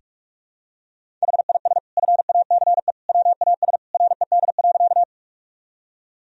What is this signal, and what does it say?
Morse code, heard as a keyed tone, about 46 wpm, 700 Hz. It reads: HIS FACE WAS RED.
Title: Sentences from Top 1000 Words - Review Part 2 - 45wpm